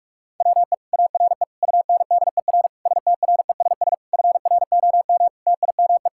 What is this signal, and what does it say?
Morse code, heard as a keyed tone, about 46 wpm, 700 Hz. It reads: WE ARE UNDER STRESS FROM TIME
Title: Sentences from Top 1000 Words - Review Part 6 - 45wpm